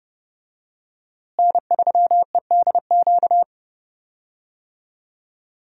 Morse code, heard as a keyed tone, about 30 wpm, 700 Hz. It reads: N3EDQ
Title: USA Call Signs - 5 Characters - Repeated 2x - 30wpm